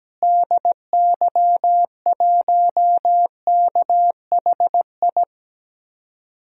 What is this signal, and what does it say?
Morse code, heard as a keyed tone, about 17 wpm, 700 Hz. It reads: DY1KHI
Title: DXCC-100 Call Signs - 6 Characters - Repeated 2x - 17wpm